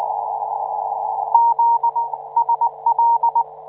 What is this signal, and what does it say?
Morse code, heard as a keyed tone, about 19 wpm, spaced as for 17 wpm, 930 Hz. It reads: ZSL